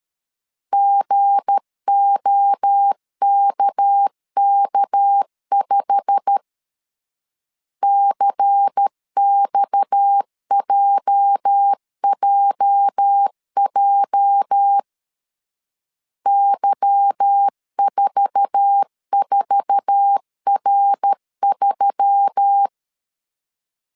Morse code, heard as a keyed tone, about 13 wpm, 800 Hz. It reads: GOKK5 CXJJJ Y44R3
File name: CW / DET / 1 9 1500C S B 12wpm 3819